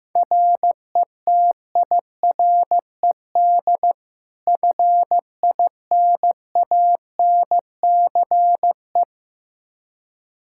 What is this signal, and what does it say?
Morse code, heard as a keyed tone, about 15 wpm, 700 Hz. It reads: RETIRED FINANCE
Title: Practice QSO - Exchange 3 - More Info - 15wpm